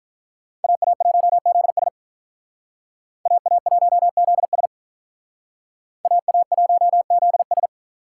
Morse code, heard as a keyed tone, about 44 wpm, 700 Hz. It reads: AA1ZS AA1ZS AA1ZS